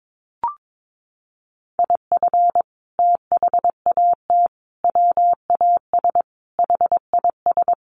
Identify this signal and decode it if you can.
Morse code, heard as a keyed tone, about 22 wpm, 700 Hz. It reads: IF THAT WAS HIS
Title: Sentences from Top 1400 Words 22wpm